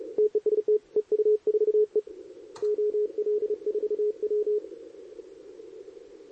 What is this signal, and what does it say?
Morse code, heard as a keyed tone, about 32 wpm, 440 Hz. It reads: TEST EU4E OL4W
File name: CQWWCW2023 / SP